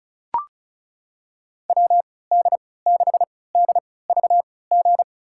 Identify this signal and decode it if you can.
Morse code, heard as a keyed tone, about 36 wpm, spaced as for 20 wpm, 700 Hz. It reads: WD6DVG
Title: USA Call Signs - 6 Characters - 3x Character Spacing - 35wpm